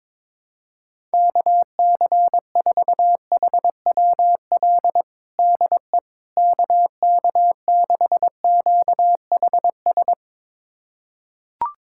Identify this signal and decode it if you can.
Morse code, heard as a keyed tone, about 22 wpm, 700 Hz. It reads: KC4HWL DE KK6QHS
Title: USA Callsign DE Callsign 22wpm